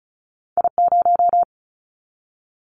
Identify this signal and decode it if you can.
Morse code, heard as a keyed tone, about 36 wpm, 700 Hz. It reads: I0
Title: Callsign Prefixes for Top 25 DXCC - 35wpm